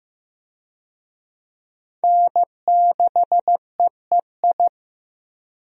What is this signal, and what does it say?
Morse code, heard as a keyed tone, about 15 wpm, 700 Hz. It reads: N6EEI
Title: USA Call Signs - 5 Characters - Repeated 2x - 15wpm